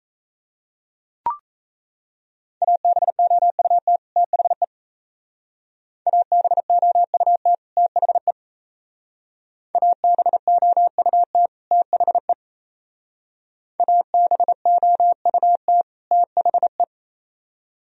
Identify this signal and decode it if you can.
Morse code, heard as a keyed tone, about 36 wpm, 700 Hz. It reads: ABOUT THE ABOUT THE ABOUT THE ABOUT THE